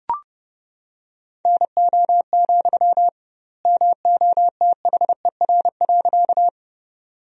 Morse code, heard as a keyed tone, about 29 wpm, 700 Hz. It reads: NO, MOTHER.
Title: 2x Word-Spacing - Sentences from Top 200 Words - 30wpm